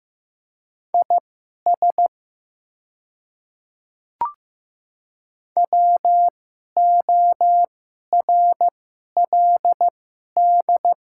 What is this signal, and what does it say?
Morse code, heard as a keyed tone, about 15 wpm, spaced as for 11 wpm, 700 Hz. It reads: IS WORLD
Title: 2x Character Spacing - Top 300 Words - 15wpm